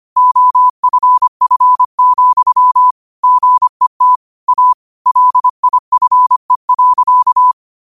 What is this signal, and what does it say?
Morse code, heard as a keyed tone, about 25 wpm, 1000 Hz. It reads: OFF, GET A LIFE.
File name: qotdfiles / 25wpm